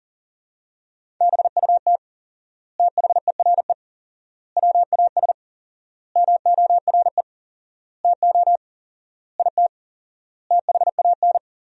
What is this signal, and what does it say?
Morse code, heard as a keyed tone, about 40 wpm, 700 Hz. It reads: BUT THERE WAS MORE TO IT THAN
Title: 4x Word-Spacing - Sentences from Top 100 Words - 40wpm